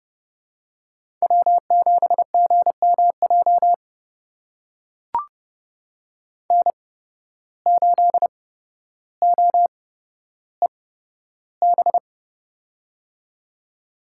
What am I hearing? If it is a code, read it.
Morse code, heard as a keyed tone, about 30 wpm, spaced as for 13 wpm, 700 Hz. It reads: W7GMJ N8OEB
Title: USA Call Signs - 5 Characters - 8x Character Spacing - 30wpm